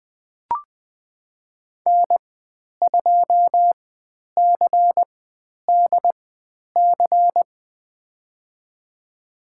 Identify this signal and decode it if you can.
Morse code, heard as a keyed tone, about 20 wpm, spaced as for 10 wpm, 700 Hz. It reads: N2CDC
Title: 102 Lesson 17g - Letter u - Call Signs 20wpm